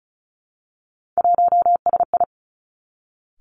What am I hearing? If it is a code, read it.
Morse code, heard as a keyed tone, about 36 wpm, 700 Hz. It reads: 1SI
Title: Three Challenging Letter-Number Combinations 35wpm